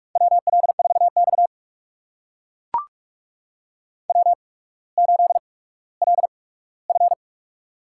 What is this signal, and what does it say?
Morse code, heard as a keyed tone, about 44 wpm, spaced as for 19 wpm, 690 Hz. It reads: WL4X W8LF
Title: USA Call Signs - 4 Characters - 8x Character Spacing - 45wpm